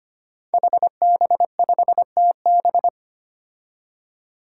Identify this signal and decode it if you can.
Morse code, heard as a keyed tone, about 25 wpm, 700 Hz. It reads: HB5TB
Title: Five Challenging Letter-Number Combinations 25wpm